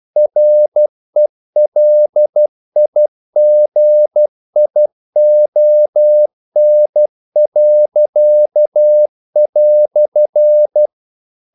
Morse code, heard as a keyed tone, about 12 wpm, 600 Hz. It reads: RELIGION."
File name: CW / Practice / EnglishQuotations0112